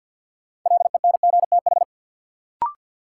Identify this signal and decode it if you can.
Morse code, heard as a keyed tone, about 52 wpm, 700 Hz. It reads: LENGTH
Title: Sentences from Top 1000 Words - New Word Drill - 50wpm